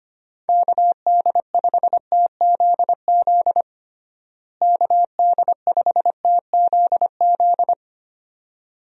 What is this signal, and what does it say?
Morse code, heard as a keyed tone, about 25 wpm, 700 Hz. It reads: KD5TZZ KD5TZZ
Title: USA Call Signs - 6 Characters - Repeated 2x - 25wpm